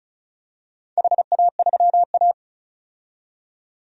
Morse code, heard as a keyed tone, about 36 wpm, 700 Hz. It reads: HA3A
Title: DXCC-100 Call Signs - 4 Characters - 35wpm